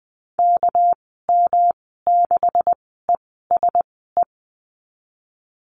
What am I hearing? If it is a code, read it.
Morse code, heard as a keyed tone, about 20 wpm, spaced as for 14 wpm, 700 Hz. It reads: KM6ESE
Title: 213 Lesson 34g - Number 8 - Call Signs 20wpm